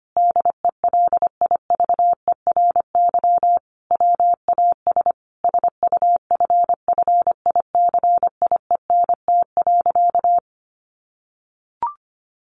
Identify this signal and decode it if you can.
Morse code, heard as a keyed tone, about 25 wpm, 700 Hz. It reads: DELIVERY WAS SUFFICIENT.
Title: Sentences from Top 1500 Words 25wpm